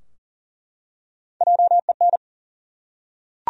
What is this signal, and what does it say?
Morse code, heard as a keyed tone, about 37 wpm, 700 Hz. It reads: JEN
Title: USA Common Names - Max 4 Letters - 40wpm